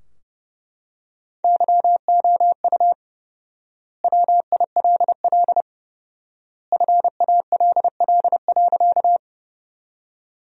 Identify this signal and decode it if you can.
Morse code, heard as a keyed tone, about 30 wpm, 700 Hz. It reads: YOU WILL FALL.
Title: 4x Word-Spacing - Sentences from Top 500 Words - 30wpm